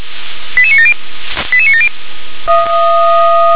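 Five-tone selective call (ZVEI), heard as quick runs of five tones, tones of about 70 ms. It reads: 88970 88970
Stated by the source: ZVEI five-tone call 88977 received with an SDR.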